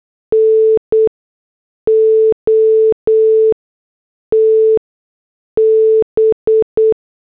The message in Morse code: NOT B